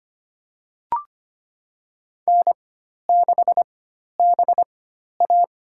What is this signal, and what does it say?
Morse code, heard as a keyed tone, about 25 wpm, spaced as for 12 wpm, 700 Hz. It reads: N6BA
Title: USA Call Signs - 4 Characters - 4x Character Spacing - 25wpm